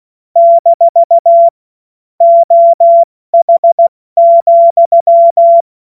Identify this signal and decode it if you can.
Morse code, heard as a keyed tone, about 16 wpm, spaced as for 14 wpm, 680 Hz. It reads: - OH,
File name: Morsekurs / Witze / Witz 13. Juli 2021, 70 F80